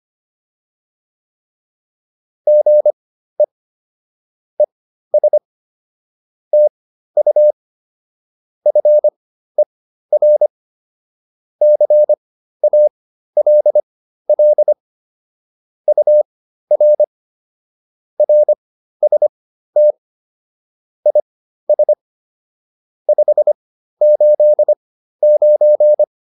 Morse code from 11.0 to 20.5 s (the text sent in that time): CALL UR RST